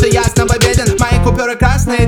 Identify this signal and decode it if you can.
There is no code.